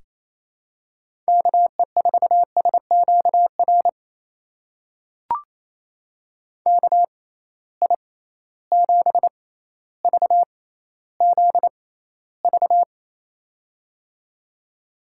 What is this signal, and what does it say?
Morse code, heard as a keyed tone, about 28 wpm, spaced as for 14 wpm, 700 Hz. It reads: KE4SQR KI7VZV
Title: USA Call Signs - 6 Characters - 6x Character Spacing - 28wpm